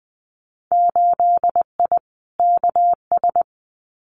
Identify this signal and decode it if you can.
Morse code, heard as a keyed tone, about 20 wpm, 700 Hz. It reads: 8I KS
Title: Winter Field Day Exchanges Encore 20wpm